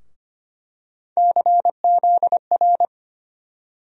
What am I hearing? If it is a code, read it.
Morse code, heard as a keyed tone, about 25 wpm, 700 Hz. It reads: CZR
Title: USA Call Sign Postfixes - 3 Letters 25wpm